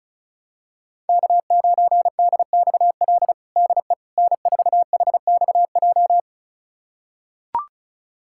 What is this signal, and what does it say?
Morse code, heard as a keyed tone, about 36 wpm, 700 Hz. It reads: K9DXL DE N4HXJ
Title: USA Callsign DE Callsign - 5 Characters - 35wpm